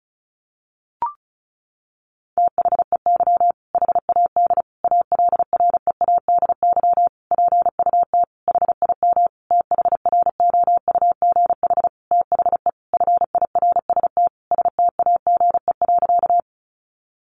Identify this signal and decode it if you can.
Morse code, heard as a keyed tone, about 34 wpm, 700 Hz. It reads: THEY HAD ALREADY PUT HIM THROUGH THE FIRST STAGE.
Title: Sentences from Top 1200 Words 35wpm